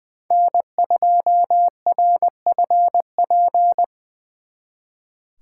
Morse code, heard as a keyed tone, about 24 wpm, 700 Hz. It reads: N2RFP
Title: Speed Racing - USA 1x3 Call Signs 20wpm